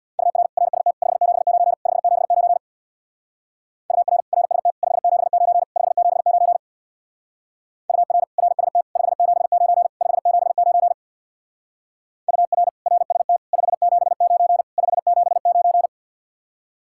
Morse code, heard as a keyed tone, about 62 wpm, 700 Hz. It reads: UR RST 579 579 UR RST 579 579 UR RST 579 579 UR RST 579 579
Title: Speed-Racing Common QSO Phrases Encore - 50wpm